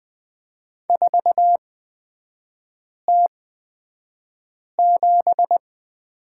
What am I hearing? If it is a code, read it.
Morse code, heard as a keyed tone, about 20 wpm, spaced as for 5 wpm, 700 Hz. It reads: 4T7
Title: 243 Lesson 39e - Letter z - Three Characters 20wpm